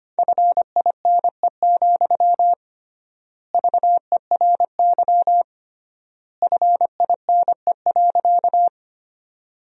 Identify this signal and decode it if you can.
Morse code, heard as a keyed tone, about 25 wpm, 700 Hz. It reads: FINE, VERY FINE.
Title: 3x Word-Spacing - Sentences from Top 500 Words - 25wpm